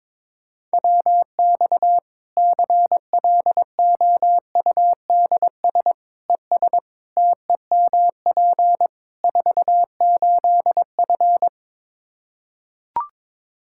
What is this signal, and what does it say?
Morse code, heard as a keyed tone, about 22 wpm, 700 Hz. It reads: WX CLOUDS ES TEMP 48F
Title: Practice QSO - Exchange 2 - Info - 22wpm